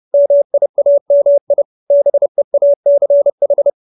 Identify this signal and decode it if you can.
Morse code, heard as a keyed tone, about 30 wpm, 570 Hz. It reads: MIAMI BEACH